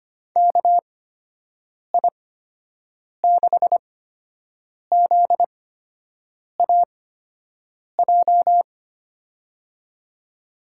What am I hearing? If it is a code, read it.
Morse code, heard as a keyed tone, about 25 wpm, spaced as for 7 wpm, 700 Hz. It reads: KI6ZAJ
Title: USA Call Signs - 6 Characters - 8x Character Spacing - 25wpm